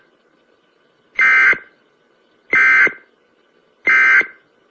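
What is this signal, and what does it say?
Emergency Alert System (SAME) tones, heard as harsh buzzing bursts. It is heard from 1.1 to 4.4 s.
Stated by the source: Emergency broadcast alert tones offered as a ringtone.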